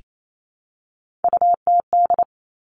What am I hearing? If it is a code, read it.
Morse code, heard as a keyed tone, about 28 wpm, 690 Hz. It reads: UTD
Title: Three Challenging Letter-Number Combinations 28wpm